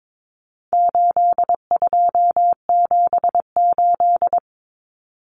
Morse code, heard as a keyed tone, about 22 wpm, 700 Hz. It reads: 8278
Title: Numbers - 4-Digit 22wpm